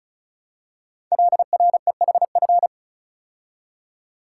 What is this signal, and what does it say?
Morse code, heard as a keyed tone, about 36 wpm, 700 Hz. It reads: LREHF
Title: Code Groups - 5 Letters - 35wpm